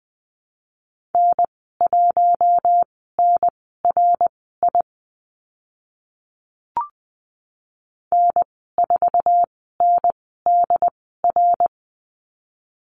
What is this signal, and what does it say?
Morse code, heard as a keyed tone, about 20 wpm, spaced as for 15 wpm, 700 Hz. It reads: N1NRI N4NDR
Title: 069 Lesson 12g - Letter d - Call Signs 20wpm